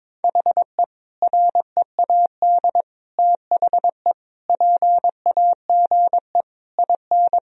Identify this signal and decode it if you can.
Morse code, heard as a keyed tone, about 21 wpm, 700 Hz. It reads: HE READ THE PAGE IN
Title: Sentences from Top 1000 Words - Review Part 1 - 22wpm